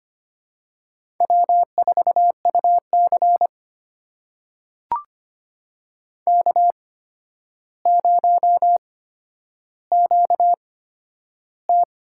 Morse code, heard as a keyed tone, about 25 wpm, spaced as for 11 wpm, 700 Hz. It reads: W4UC K0QT
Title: USA Call Signs - 4 Characters - 8x Character Spacing - 25wpm